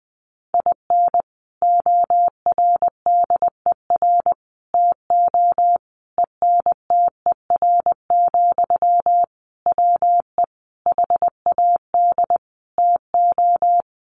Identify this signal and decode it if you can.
Morse code, heard as a keyed tone, about 20 wpm, 700 Hz. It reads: IN ORDER TO ENTER, WE HAD TO
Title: Sentences from Top 1300 Words 20wpm